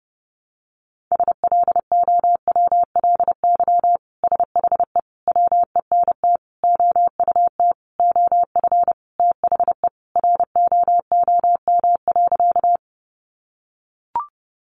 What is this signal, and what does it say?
Morse code, heard as a keyed tone, about 30 wpm, 700 Hz. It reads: SLOWLY SHE WENT OUT OF THE ROOM.
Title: Sentences from Top 1100 Words 30wpm